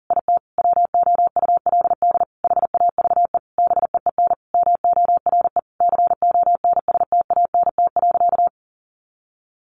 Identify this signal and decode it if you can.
Morse code, heard as a keyed tone, about 40 wpm, 700 Hz. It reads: IT WOULD HAVE BEEN MORE CONSTANT.
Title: Sentences from Top 1500 Words 40wpm